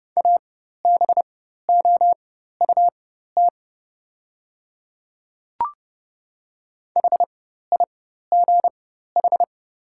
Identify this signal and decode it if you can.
Morse code, heard as a keyed tone, about 30 wpm, spaced as for 14 wpm, 700 Hz. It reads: ABOUT HIGH
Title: 4x Character Spacing - Top 200 Words - 30wpm